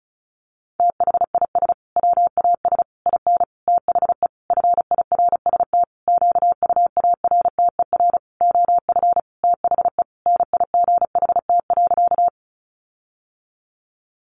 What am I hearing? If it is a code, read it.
Morse code, heard as a keyed tone, about 34 wpm, 700 Hz. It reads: THIS WAS IN THE FIRST QUARTER OF THE NIGHT.
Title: Sentences from Top 1300 Words 35wpm